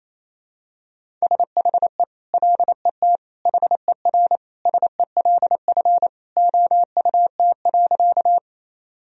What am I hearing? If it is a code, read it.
Morse code, heard as a keyed tone, about 28 wpm, 700 Hz. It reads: SHE LET HER SELF OUT.
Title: Sentences from Top 1000 Words - Review Part 1 - 28wpm